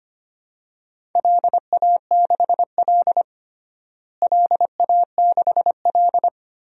Morse code, heard as a keyed tone, about 25 wpm, 700 Hz. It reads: LA6L LA6L
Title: DXCC-100 Call Signs - 4 Characters - Repeated 3x - 25wpm